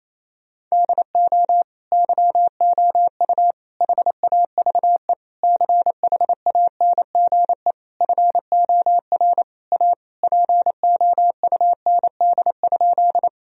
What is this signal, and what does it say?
Morse code, heard as a keyed tone, about 28 wpm, 700 Hz. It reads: DO YOU HAVE CHANGE FOR A POUND?
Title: Sentences from Top 1000 Words - Review Part 2 - 28wpm